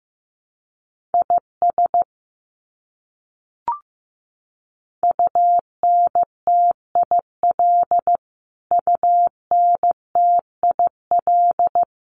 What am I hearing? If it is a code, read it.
Morse code, heard as a keyed tone, about 15 wpm, 700 Hz. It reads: IS UNTIL UNTIL
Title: Top 500 Words Repeated 15wpm